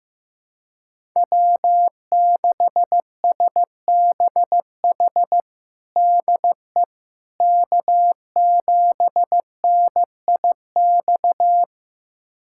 Morse code, heard as a keyed tone, about 15 wpm, 700 Hz. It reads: W6SBH DE K7NIX
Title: USA Callsign DE Callsign 15wpm